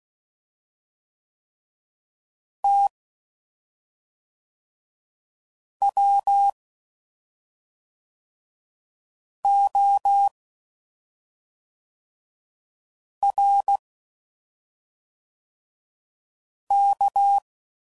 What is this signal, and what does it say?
Morse code, heard as a keyed tone, about 16 wpm, 800 Hz. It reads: T W O R K
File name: Suffixes / Special ING's / SlowToFaster ING's